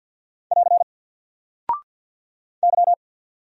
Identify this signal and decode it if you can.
Morse code, heard as a keyed tone, about 50 wpm, 700 Hz. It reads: <AR> Y
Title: Single Character (Letter:Number:Punctuation:Prosign) 50wpm